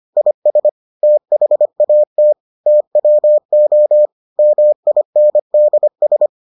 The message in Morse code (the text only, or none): IS THAT TWO MINDS